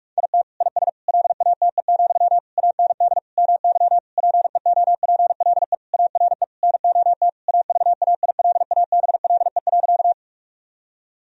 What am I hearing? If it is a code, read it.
Morse code, heard as a keyed tone, about 46 wpm, 700 Hz. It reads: IT IS LATE, AND MY PEOPLE ARE NOT AVAILABLE.
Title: Sentences from Top 1000 Words - Review Part 3 - 45wpm